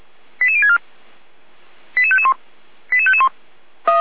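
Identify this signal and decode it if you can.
Five-tone selective call (ZVEI), heard as quick runs of five tones, tones of about 70 ms. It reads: 88064 88551 88551
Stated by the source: ZVEI five-tone selective call received via SDR.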